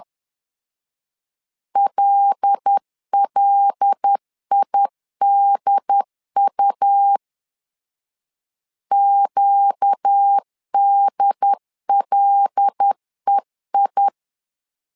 Morse code, heard as a keyed tone, about 10 wpm, 800 Hz. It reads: LLIDU QDLEI
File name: LEZIONI / 5 LBVC / 5 L 10wpm 2259